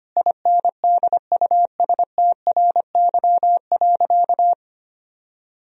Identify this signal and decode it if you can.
Morse code, heard as a keyed tone, about 25 wpm, 700 Hz. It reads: INDUSTRY.
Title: Sentences from Top 1000 Words - Review Part 4 - 25wpm